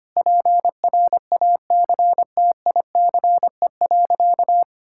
Morse code, heard as a keyed tone, about 25 wpm, 700 Hz. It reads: PRACTICE.
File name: practice / 25wpm / Sentences with 9-Word Phrases - 25wpm